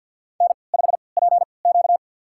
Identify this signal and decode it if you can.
Morse code, heard as a keyed tone, about 50 wpm, spaced as for 28 wpm, 700 Hz. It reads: N5PQ
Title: USA Call Signs - 4 Characters - 3x Character Spacing - 50wpm